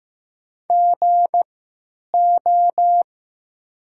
Morse code, heard as a keyed tone, about 15 wpm, spaced as for 9 wpm, 700 Hz. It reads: GO
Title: 3x Character Spacing - Top 300 Words - 15wpm